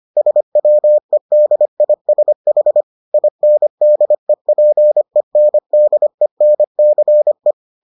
Morse code, heard as a keyed tone, about 25 wpm, 600 Hz. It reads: SWEDISH INDEPENDENCE